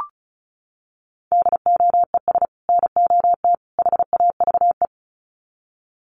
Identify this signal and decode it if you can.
Morse code, heard as a keyed tone, about 34 wpm, 700 Hz. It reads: DOES NOT HAVE
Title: Sets of 3 Words 35wpm